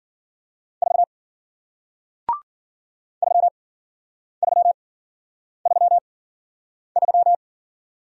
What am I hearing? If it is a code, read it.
Morse code, heard as a keyed tone, about 50 wpm, spaced as for 9 wpm, 700 Hz. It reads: 4 3333